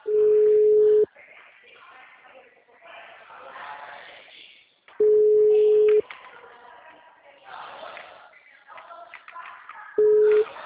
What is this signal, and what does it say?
Ringback tone (Europe, 425 Hz), with speech.
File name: out- -2020- - - .40298